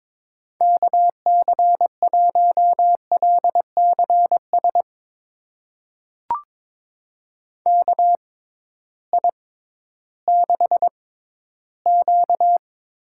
Morse code, heard as a keyed tone, about 22 wpm, spaced as for 13 wpm, 700 Hz. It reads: KC1LCS KI6Q
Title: USA Call Signs - 6 Characters - 6x Character Spacing - 22wpm